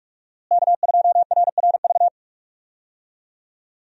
Morse code, heard as a keyed tone, about 46 wpm, 700 Hz. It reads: K2RRV
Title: USA Call Signs - 5 Characters - Repeated 3x - 45wpm